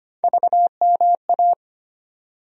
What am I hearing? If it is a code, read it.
Morse code, heard as a keyed tone, about 25 wpm, 700 Hz. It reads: VMA
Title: USA Call Sign Postfixes - 3 Letters 25wpm